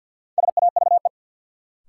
Morse code, heard as a keyed tone, about 52 wpm, 700 Hz. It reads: SAVE
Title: Sentences from Top 1100 Words - New Word Drill - 50wpm